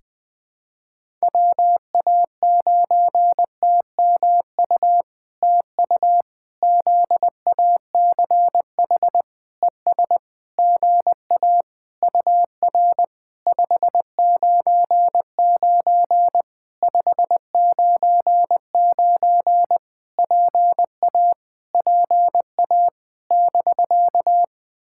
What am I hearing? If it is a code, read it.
Morse code, heard as a keyed tone, about 20 wpm, 700 Hz. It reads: WA9TMU TU ZACH ES GA UR 599 599 PA PA <BK>
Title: POTA QSO Practice - 20wpm